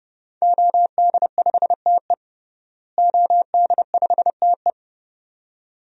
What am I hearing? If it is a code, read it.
Morse code, heard as a keyed tone, about 30 wpm, 700 Hz. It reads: OD5TE OD5TE
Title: DXCC-100 Call Signs - 5 Characters - Repeated 2x - 30wpm